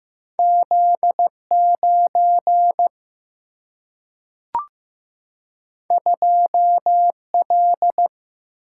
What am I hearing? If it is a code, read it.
Morse code, heard as a keyed tone, about 15 wpm, 700 Hz. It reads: Z 9 2 L
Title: Single-Letter Single-Number 15wpm